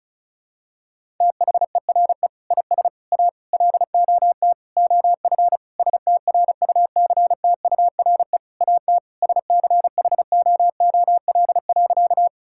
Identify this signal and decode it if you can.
Morse code, heard as a keyed tone, about 36 wpm, 700 Hz. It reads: THERE IS A LOT OF STRUCTURE AT SCHOOL.
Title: Sentences from Top 1000 Words - Review Part 6 - 35wpm